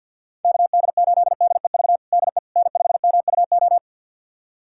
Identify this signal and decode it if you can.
Morse code, heard as a keyed tone, about 47 wpm, 700 Hz. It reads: KD8DEV DE N5MUO
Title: USA Callsign DE Callsign 50wpm